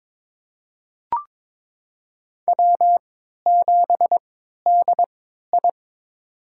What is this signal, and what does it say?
Morse code, heard as a keyed tone, about 23 wpm, spaced as for 13 wpm, 700 Hz. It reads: W7DI
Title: USA Call Signs - 3x Character Spacing - 22wpm